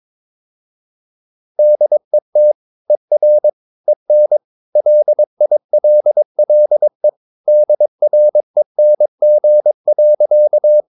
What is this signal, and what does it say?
Morse code, heard as a keyed tone, about 22 wpm, 600 Hz. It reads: DET ER EN LILLE DRENG.
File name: Ferie QRS 22wpm